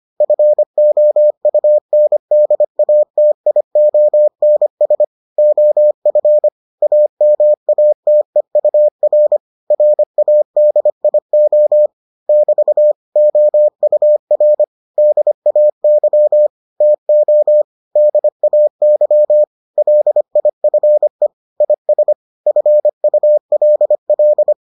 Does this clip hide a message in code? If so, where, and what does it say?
Yes: Morse code from 0.2 to 24.5 s, heard as a keyed tone, about 25 wpm, 600 Hz. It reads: FOUNDATIONS OF AMATEUR RADIO <BT> OUR DAY TO DAY LIFE IS FULL